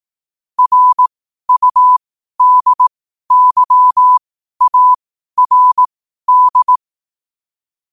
Morse code, heard as a keyed tone, about 18 wpm, spaced as for 12 wpm, 1000 Hz. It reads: RUDYARD
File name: qotdfiles / 12wpm